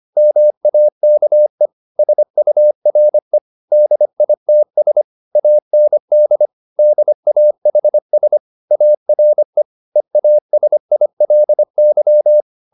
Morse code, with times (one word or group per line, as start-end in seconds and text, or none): 0.2-1.7 MAKE
2.0-3.4 SURE
3.7-5.0 DITS
5.4-6.5 AND
6.8-8.4 DAHS
8.7-9.6 ARE
10.0-12.4 EASILY